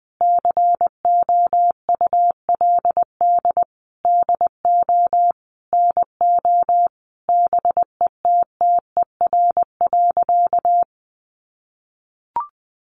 Morse code, heard as a keyed tone, about 20 wpm, 700 Hz. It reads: COULD DO NO BETTER.
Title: Sentences from Top 1300 Words 20wpm